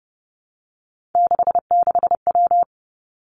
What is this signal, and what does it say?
Morse code, heard as a keyed tone, about 30 wpm, 700 Hz. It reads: 66W
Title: Three Challenging Letter-Number Combinations 30wpm